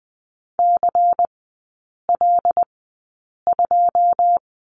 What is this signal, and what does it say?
Morse code, heard as a keyed tone, about 20 wpm, spaced as for 8 wpm, 690 Hz. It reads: CL2
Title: 093 Lesson 16e - Letter c - Three Characters 20wpm